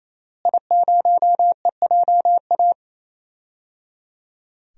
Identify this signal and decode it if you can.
Morse code, heard as a keyed tone, about 28 wpm, 700 Hz. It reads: I0EJA
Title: DXCC-100 Call Signs - 5 Characters - Repeated 3x - 28wpm